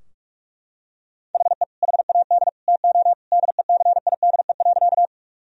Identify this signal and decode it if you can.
Morse code, heard as a keyed tone, about 43 wpm, 700 Hz. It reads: HE HAD TO DECIDE.
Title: Sentences from Top 1000 Words - Review Part 3 - 45wpm